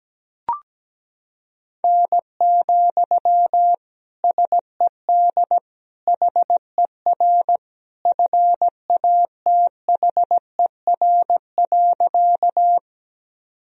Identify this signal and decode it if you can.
Morse code, heard as a keyed tone, about 17 wpm, 700 Hz. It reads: N, SED HER FATHER.
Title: Sentences from Top 300 Words and Abbreviations 17wpm